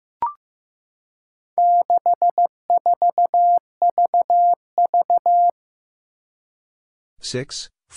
Morse code, heard as a keyed tone, about 15 wpm, 700 Hz. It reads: 64VV
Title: Four Challenging Letter-Number Combinations 15wpm